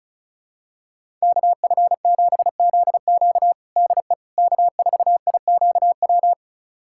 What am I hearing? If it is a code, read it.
Morse code, heard as a keyed tone, about 36 wpm, 700 Hz. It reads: KF7ZQ DE K4IQW
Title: USA Callsign DE Callsign - 5 Characters - 35wpm